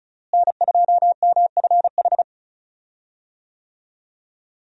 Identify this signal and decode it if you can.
Morse code, heard as a keyed tone, about 34 wpm, 700 Hz. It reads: N2MFH
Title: USA Call Signs - 5 Characters - Repeated 2x - 35wpm